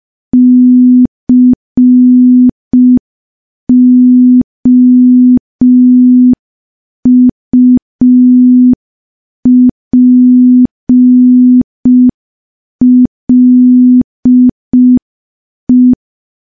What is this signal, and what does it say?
Morse code, heard as a keyed tone, about 5 wpm, 250 Hz. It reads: COUPLE
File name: morse / morse-training-05WPM